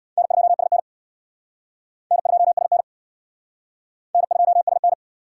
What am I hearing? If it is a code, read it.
Morse code, heard as a keyed tone, about 59 wpm, 700 Hz. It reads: N2SN N2SN N2SN
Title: Speed Racing - USA 1x2 Call Signs 50wpm